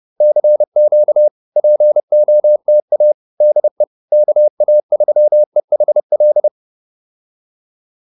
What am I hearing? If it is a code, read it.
Morse code, heard as a keyed tone, about 30 wpm, 590 Hz. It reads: CQ POTA DE KA3EHL
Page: advanced-practice-files